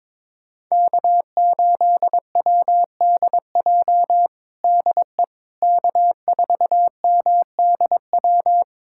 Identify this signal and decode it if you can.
Morse code, heard as a keyed tone, about 22 wpm, 700 Hz. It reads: K8WDJ DE K4MDW
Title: USA Callsign DE Callsign - 5 Characters - 22wpm